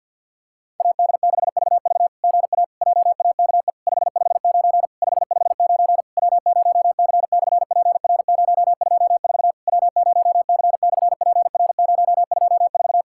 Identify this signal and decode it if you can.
Morse code, heard as a keyed tone, about 50 wpm, 700 Hz. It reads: AD6VV GA JAKE 559 559 W0C/PR024 W0C/PR024
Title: SOTA QSO Practice 50wpm